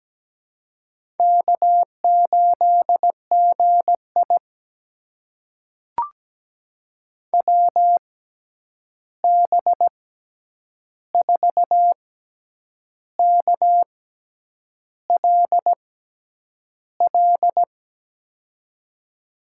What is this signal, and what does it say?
Morse code, heard as a keyed tone, about 17 wpm, spaced as for 8 wpm, 700 Hz. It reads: K8GI WB4KLL
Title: USA Call Signs - 6x Character Spacing - 17wpm